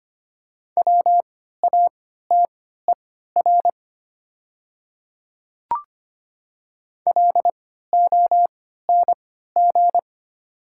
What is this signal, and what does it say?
Morse code, heard as a keyed tone, about 25 wpm, spaced as for 14 wpm, 700 Hz. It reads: WATER LONG
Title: 3x Character Spacing - Top 100 Words - 25wpm